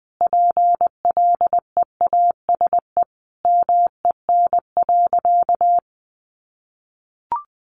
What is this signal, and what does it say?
Morse code, heard as a keyed tone, about 20 wpm, 700 Hz. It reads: PLEASE MEN.
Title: Sentences from Top 1500 Words 20wpm